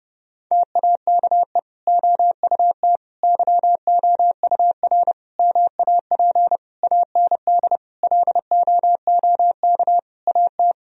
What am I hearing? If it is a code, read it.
Morse code, heard as a keyed tone, about 30 wpm, 700 Hz. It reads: TAKE OUT YOUR MAP AND LOOK AT
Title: Sentences from Top 1000 Words - Review Part 2 - 30wpm